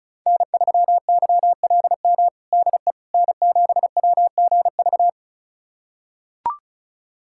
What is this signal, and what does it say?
Morse code, heard as a keyed tone, about 34 wpm, 700 Hz. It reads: N3YLM DE N7WGV
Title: USA Callsign DE Callsign - 5 Characters - 35wpm